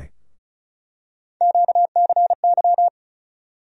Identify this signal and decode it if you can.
Morse code, heard as a keyed tone, about 34 wpm, 700 Hz. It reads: QCY
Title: USA Call Sign Postfixes - 3 Letters 35wpm